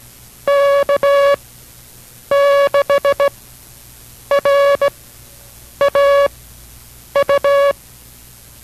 Morse code, heard as a keyed tone, about 17 wpm, spaced as for 7 wpm, 560 Hz. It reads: K6RAU